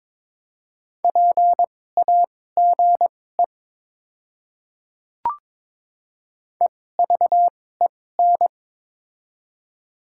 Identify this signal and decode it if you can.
Morse code, heard as a keyed tone, about 22 wpm, spaced as for 16 wpm, 700 Hz. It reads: PAGE EVEN
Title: 2x Character Spacing - Top 300 Words - 22wpm